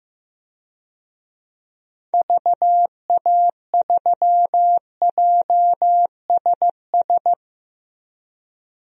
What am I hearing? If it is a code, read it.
Morse code, heard as a keyed tone, about 15 wpm, 700 Hz. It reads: VA3JSS
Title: DXCC-100 Call Signs - 6 Characters - 15wpm